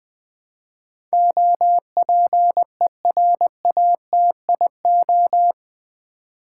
Morse code, heard as a keyed tone, about 20 wpm, 700 Hz. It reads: OPERATIO
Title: Build A Word - 4 Letters At A Time - 11-Letter Words - 20wpm